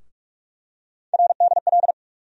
Morse code, heard as a keyed tone, about 44 wpm, 700 Hz. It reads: RDL
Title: USA Call Sign Postfixes - 3 Letters 45wpm